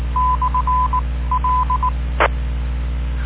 Morse code, heard as a keyed tone, about 19 wpm, spaced as for 15 wpm, 1010 Hz. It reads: /L